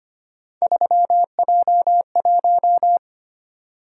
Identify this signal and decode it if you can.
Morse code, heard as a keyed tone, about 25 wpm, 700 Hz. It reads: 3J1
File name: Three Letter-Number Combinations - 25wpm